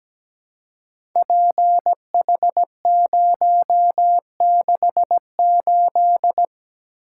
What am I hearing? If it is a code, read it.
Morse code, heard as a keyed tone, about 17 wpm, 700 Hz. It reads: PH068
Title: Code Groups - 5 Letter:Numbers - 17wpm